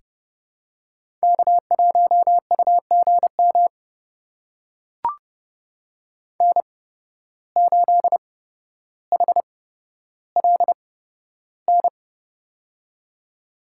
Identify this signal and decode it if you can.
Morse code, heard as a keyed tone, about 30 wpm, spaced as for 13 wpm, 700 Hz. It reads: K1UGM N8HLN
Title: USA Call Signs - 5 Characters - 8x Character Spacing - 30wpm